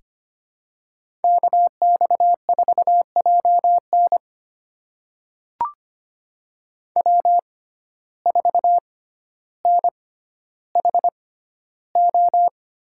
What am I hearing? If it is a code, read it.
Morse code, heard as a keyed tone, about 25 wpm, spaced as for 13 wpm, 700 Hz. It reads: KX4JN W4NHO
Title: USA Call Signs - 5 Characters - 6x Character Spacing - 25wpm